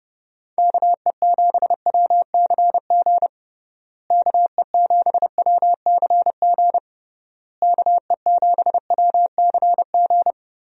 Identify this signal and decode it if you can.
Morse code, heard as a keyed tone, about 30 wpm, 700 Hz. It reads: KE7WCG KE7WCG KE7WCG
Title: USA Call Signs - 6 Characters - Repeated 3x - 30wpm